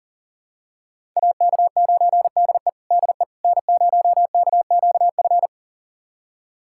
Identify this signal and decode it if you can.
Morse code, heard as a keyed tone, about 40 wpm, 700 Hz. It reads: AK9DE DE N0KQF
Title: USA Callsign DE Callsign - 5 Characters - 40wpm